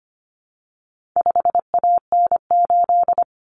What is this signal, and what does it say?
Morse code, heard as a keyed tone, about 25 wpm, 700 Hz. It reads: 5AN8
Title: Common Cut Numbers - 4-Digit 25wpm